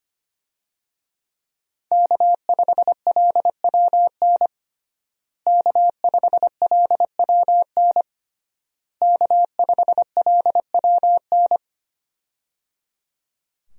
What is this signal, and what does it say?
Morse code, heard as a keyed tone, about 25 wpm, 700 Hz. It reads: K5LWN K5LWN K5LWN
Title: USA Call Signs - 5 Characters - Repeated 3x - 25wpm